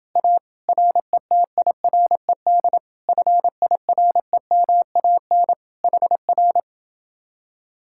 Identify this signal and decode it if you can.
Morse code, heard as a keyed tone, about 28 wpm, 700 Hz. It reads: A RETIRED FIREMAN HR